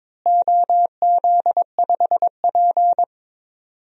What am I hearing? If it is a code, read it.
Morse code, heard as a keyed tone, about 22 wpm, 700 Hz. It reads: OZ5P
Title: DXCC-100 Call Signs - 4 Characters - Repeated 3x - 22wpm